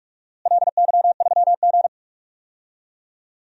Morse code, heard as a keyed tone, about 46 wpm, 700 Hz. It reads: LY3G
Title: DXCC-100 Call Signs - 4 Characters - Repeated 2x - 45wpm